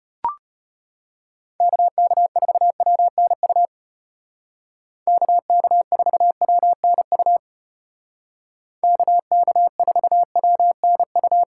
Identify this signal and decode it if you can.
Morse code, heard as a keyed tone, about 33 wpm, 700 Hz. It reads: KK4WNU KK4WNU KK4WNU